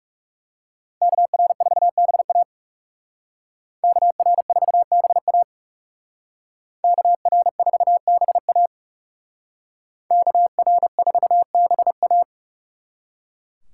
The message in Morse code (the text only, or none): KR4BA KR4BA KR4BA KR4BA